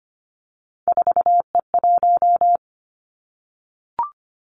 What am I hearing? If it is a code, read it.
Morse code, heard as a keyed tone, about 25 wpm, 700 Hz. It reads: 4E1
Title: Three Challenging Letter-Number Combinations 25wpm